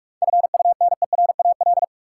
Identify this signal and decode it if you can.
Morse code, heard as a keyed tone, about 43 wpm, 700 Hz. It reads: FUNERAL
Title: Seven Letter Words - 45wpm